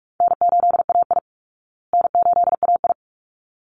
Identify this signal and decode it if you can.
Morse code, heard as a keyed tone, about 44 wpm, 700 Hz. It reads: N8AI N8AI
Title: Call Signs Repeated 3x - 45wpm